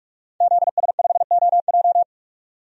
Morse code, heard as a keyed tone, about 45 wpm, 700 Hz. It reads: ZS5OJ
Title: DXCC-100 Call Signs - 5 Characters - 45wpm